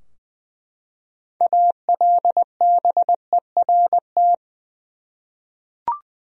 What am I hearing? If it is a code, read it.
Morse code, heard as a keyed tone, about 20 wpm, 700 Hz. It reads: ALBERT
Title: USA Common Names - 20wpm